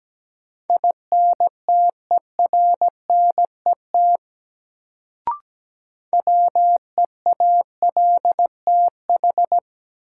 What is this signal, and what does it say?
Morse code, heard as a keyed tone, about 17 wpm, 700 Hz. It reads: INTERNET WEALTH
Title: Sentences from Top 1000 Words - Review Part 4 - New Word Drill - 17wpm